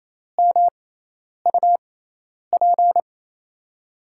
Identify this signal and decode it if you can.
Morse code, heard as a keyed tone, about 28 wpm, spaced as for 10 wpm, 700 Hz. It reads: MUP
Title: USA Call Signs - 6x Character Spacing - 28wpm